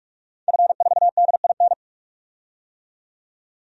Morse code, heard as a keyed tone, about 46 wpm, 700 Hz. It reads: F4DIN
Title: DXCC-100 Call Signs - 5 Characters - 45wpm